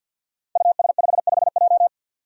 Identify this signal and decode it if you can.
Morse code, heard as a keyed tone, about 46 wpm, 700 Hz. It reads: US55J
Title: Five Challenging Letter-Number Combinations 50wpm